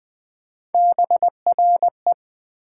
Morse code, heard as a keyed tone, about 21 wpm, 700 Hz. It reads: BRE
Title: Build A Word - 3 Letters At A Time - 6-Letter Words - 20wpm